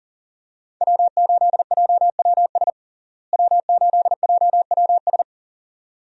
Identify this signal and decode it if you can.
Morse code, heard as a keyed tone, about 37 wpm, 700 Hz. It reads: W8JWS W8JWS
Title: USA Call Signs - 5 Characters - Repeated 2x - 40wpm